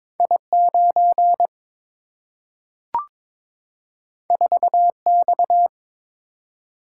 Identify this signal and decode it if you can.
Morse code, heard as a keyed tone, about 22 wpm, 700 Hz. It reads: I9 4X
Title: Single-Letter Single-Number 22wpm